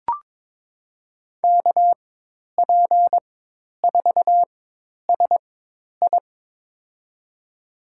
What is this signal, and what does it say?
Morse code, heard as a keyed tone, about 21 wpm, spaced as for 10 wpm, 700 Hz. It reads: KP4SI
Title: USA Call Signs - 5 Characters - 4x Character Spacing - 22wpm